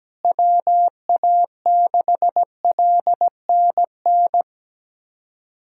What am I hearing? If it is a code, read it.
Morse code, heard as a keyed tone, about 17 wpm, 700 Hz. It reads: WA6LNN
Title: USA Callsign DE Callsign 17wpm